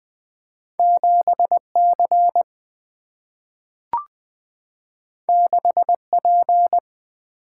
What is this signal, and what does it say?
Morse code, heard as a keyed tone, about 20 wpm, 700 Hz. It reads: 7 C 6 P
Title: Two Letter-Number Combinations - 20wpm